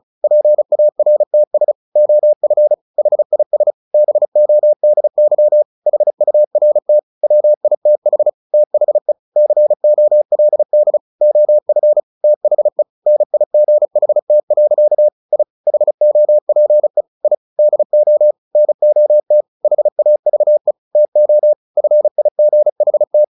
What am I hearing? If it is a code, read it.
Morse code, heard as a keyed tone, about 34 wpm, 600 Hz. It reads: PARTS OF HIS BODY HURT WITH THE COLD OF THE NIGHT. I HOPE I DO NOT HAVE TO FIGHT
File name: the old man and the sea 2 35wpm